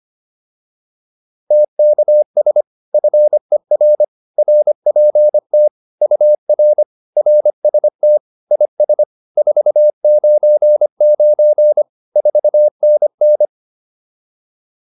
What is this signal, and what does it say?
Morse code, heard as a keyed tone, about 25 wpm, 600 Hz. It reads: TKS FER RPT UR RST IS 499 4NN